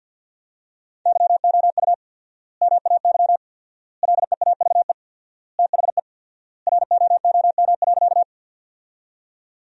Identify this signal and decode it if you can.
Morse code, heard as a keyed tone, about 50 wpm, 700 Hz. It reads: YOU MAY LEAVE THE ROOM.
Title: 4x Word-Spacing - Sentences from Top 400 Words - 50wpm